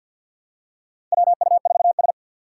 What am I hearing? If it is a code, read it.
Morse code, heard as a keyed tone, about 47 wpm, 700 Hz. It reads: WU4S
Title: USA Call Signs - 4 Characters - 3x Character Spacing - 50wpm